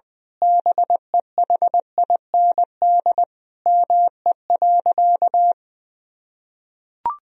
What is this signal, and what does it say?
Morse code, heard as a keyed tone, about 20 wpm, 700 Hz. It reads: BEHIND ME.
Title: Sentences from Top 1000 Words - Review Part 1 - 20wpm